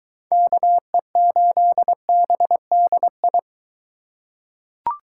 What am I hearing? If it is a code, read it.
Morse code, heard as a keyed tone, about 23 wpm, 700 Hz. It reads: KE8BDI